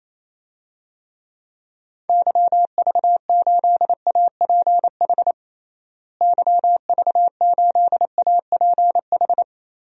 Morse code, heard as a keyed tone, about 28 wpm, 700 Hz. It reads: YV8APH YV8APH
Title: DXCC-100 Call Signs - 6 Characters - Repeated 3x - 28wpm